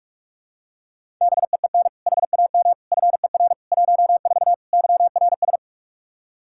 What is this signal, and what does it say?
Morse code, heard as a keyed tone, about 46 wpm, 700 Hz. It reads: BEEN HAM FER 14 YRS
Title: Practice QSO - Exchange 3 - More Info - 45wpm